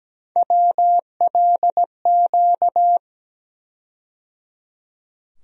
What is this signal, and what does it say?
Morse code, heard as a keyed tone, about 17 wpm, 700 Hz. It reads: WLQ
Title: USA Call Sign Postfixes - 3 Letters 17wpm